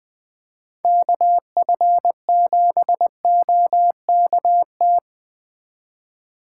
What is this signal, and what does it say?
Morse code, heard as a keyed tone, about 23 wpm, 700 Hz. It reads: KF7OKT